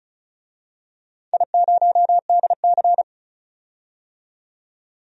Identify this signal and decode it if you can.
Morse code, heard as a keyed tone, about 34 wpm, 700 Hz. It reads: I0DC
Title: DXCC-100 Call Signs - 4 Characters - 35wpm